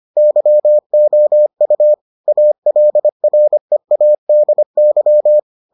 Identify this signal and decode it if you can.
Morse code, heard as a keyed tone, about 25 wpm, 600 Hz. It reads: YOU ALREADY